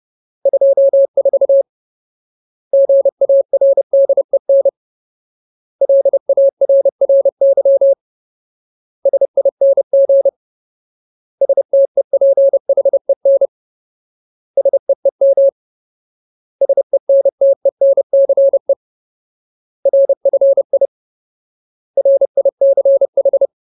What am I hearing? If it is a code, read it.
Morse code, heard as a keyed tone, about 30 wpm, 550 Hz. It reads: 24 GARDEN LARRY SING STEPHEN SEEM SENTENCE RFI RICH
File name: Ultimate Word List / 30WPM Ultimate